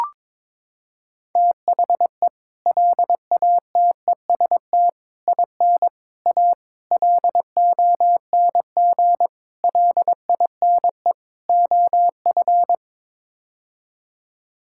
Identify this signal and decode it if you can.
Morse code, heard as a keyed tone, about 22 wpm, 700 Hz. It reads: THE LATEST IN A LONG LINE OF
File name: practice / 22wpm / Sets of 7 Words - 22wpm